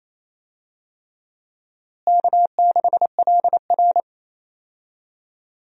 Morse code, heard as a keyed tone, about 28 wpm, 700 Hz. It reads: K6LR
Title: USA Call Signs - 4 Characters - Repeated 3x - 28wpm